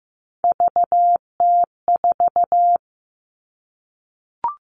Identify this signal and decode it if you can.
Morse code, heard as a keyed tone, about 15 wpm, 700 Hz. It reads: VT4
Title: All Cut Numbers - 3-Digit 15wpm